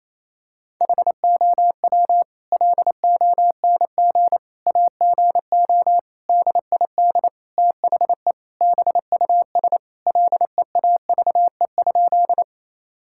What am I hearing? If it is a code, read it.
Morse code, heard as a keyed tone, about 28 wpm, 700 Hz. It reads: HOW LONG AGO DID THE BUS LEAVE?
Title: Sentences from Top 1000 Words - Review Part 6 - 28wpm